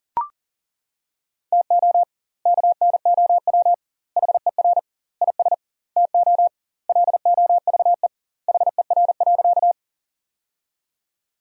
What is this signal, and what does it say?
Morse code, heard as a keyed tone, about 40 wpm, 700 Hz. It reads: TO KNOW HER IS TO LOVE HER.
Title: 2x Word-Spacing - Sentences from Top 400 Words - 40wpm